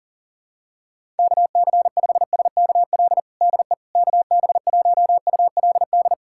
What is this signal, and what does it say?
Morse code, heard as a keyed tone, about 37 wpm, 700 Hz. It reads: KC5SKL DE KB1ULD
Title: USA Callsign DE Callsign - 6 Characters - 40wpm